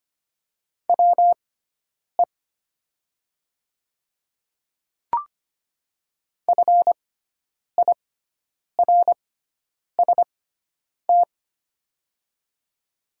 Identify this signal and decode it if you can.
Morse code, heard as a keyed tone, about 25 wpm, spaced as for 9 wpm, 700 Hz. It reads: WE FIRST
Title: 6x Character Spacing - Top 100 Words - 25wpm